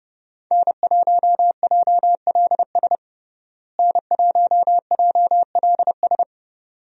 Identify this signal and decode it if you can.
Morse code, heard as a keyed tone, about 30 wpm, 700 Hz. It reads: N1JLS N1JLS
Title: USA Call Signs - 5 Characters - Repeated 2x - 30wpm